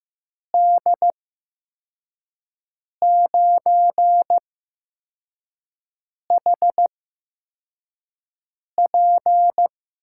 Morse code, heard as a keyed tone, about 15 wpm, spaced as for 4 wpm, 700 Hz. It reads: D9HP
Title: USA Call Signs - 5 Characters - 8x Character Spacing - 15wpm